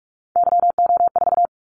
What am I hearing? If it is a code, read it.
Morse code, heard as a keyed tone, about 46 wpm, 700 Hz. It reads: YO4
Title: Callsign Prefixes for Top 25 DXCC - 45wpm